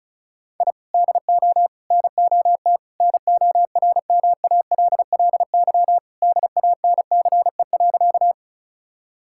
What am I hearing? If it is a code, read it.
Morse code, heard as a keyed tone, about 36 wpm, 700 Hz. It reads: I DO NOT NORMALLY DANCE.
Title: Sentences from Top 1000 Words - Review Part 6 - 35wpm